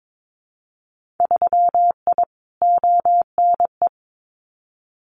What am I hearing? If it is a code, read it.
Morse code, heard as a keyed tone, about 22 wpm, 700 Hz. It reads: 3I ONE
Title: Winter Field Day Exchanges Encore 22wpm